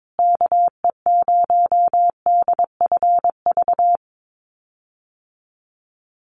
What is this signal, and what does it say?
Morse code, heard as a keyed tone, about 22 wpm, 700 Hz. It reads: KE0DFV
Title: USA Call Signs Repeated 3x 22wpm